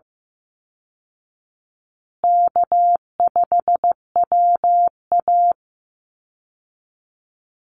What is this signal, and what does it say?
Morse code, heard as a keyed tone, about 15 wpm, 700 Hz. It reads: K5WA
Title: Easy Call Signs Repeated 2x - 15wpm